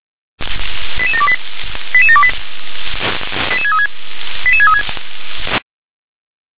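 Five-tone selective call (ZVEI), heard as quick runs of five tones, tones of about 70 ms. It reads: 88728 88728 88636 88636
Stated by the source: ZVEI five-tone selective call received on an SDR.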